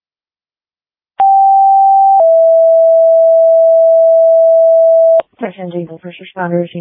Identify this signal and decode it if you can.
Two-tone page, heard as a one-second tone, then a three-second tone, about 780 then 660 Hz, with speech. It is heard from 1.2 to 5.2 s.